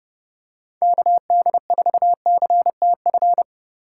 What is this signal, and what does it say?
Morse code, heard as a keyed tone, about 29 wpm, 700 Hz. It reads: KD4CTF